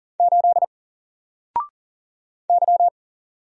Morse code, heard as a keyed tone, about 40 wpm, 700 Hz. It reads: 8 Y
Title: Single Letter-Number Alt 40wpm